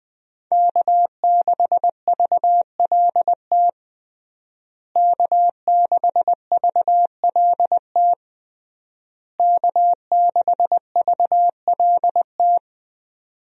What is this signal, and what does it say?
Morse code, heard as a keyed tone, about 20 wpm, 700 Hz. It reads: K6VLT K6VLT K6VLT
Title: USA Call Signs - 5 Characters - Repeated 3x - 20wpm